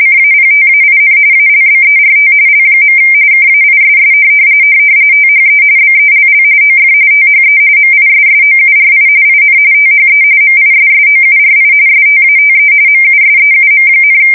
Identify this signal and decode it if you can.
RTTY, about 45.45 baud. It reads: CAPTAIN ALBERT L. KAISS THE BELL OF THE MISSOURI IN THE MISSOURI STATE MUSEUM,
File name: fsk170 / USS-Missouri